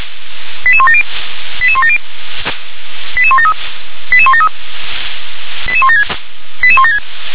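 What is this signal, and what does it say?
Five-tone selective call (ZVEI), heard as quick runs of five tones, tones of about 70 ms. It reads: 88170 88170 88173 88173 88176 88176
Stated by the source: ZVEI five-tone selective call received on SDR.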